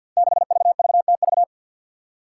Morse code, heard as a keyed tone, about 53 wpm, 700 Hz. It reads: 6VVTV
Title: Five Challenging Letter-Number Combinations 50wpm